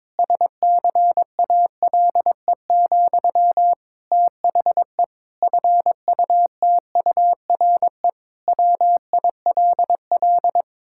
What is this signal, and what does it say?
Morse code, heard as a keyed tone, about 22 wpm, 700 Hz. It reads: SCALE, THE FUTURE WILL
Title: Sentences from Top 1000 Words - Review Part 5 - 22wpm